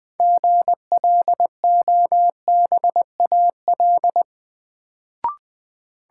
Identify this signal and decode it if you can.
Morse code, heard as a keyed tone, about 20 wpm, 700 Hz. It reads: GLOBAL
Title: Sentences from Top 1200 Words - New Word Drill Alt - 20wpm